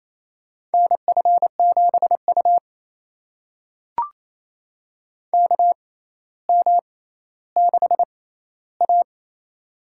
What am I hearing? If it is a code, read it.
Morse code, heard as a keyed tone, about 28 wpm, spaced as for 14 wpm, 700 Hz. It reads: NF7U KM6A
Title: USA Call Signs - 4 Characters - 6x Character Spacing - 28wpm